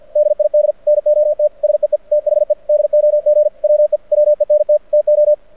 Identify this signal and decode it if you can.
Morse code, heard as a keyed tone, about 51 wpm, 600 Hz. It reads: DID NOT SEE THE DOG WE WENT TO
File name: archived / CRSnet